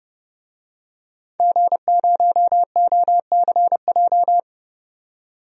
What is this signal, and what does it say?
Morse code, heard as a keyed tone, about 31 wpm, 700 Hz. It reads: G0OCJ
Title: DXCC-100 Call Signs - 5 Characters - 30wpm